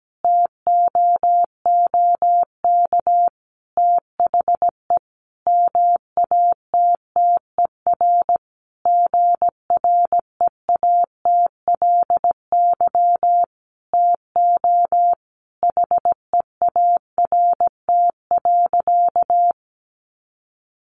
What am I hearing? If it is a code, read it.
Morse code, heard as a keyed tone, about 17 wpm, 700 Hz. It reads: TOOK THE MATTER GREATLY TO HEART.
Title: Sentences from Top 1400 Words 17wpm